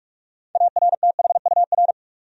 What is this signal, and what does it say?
Morse code, heard as a keyed tone, about 46 wpm, 700 Hz. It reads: ARTHUR
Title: USA Common Names - 45wpm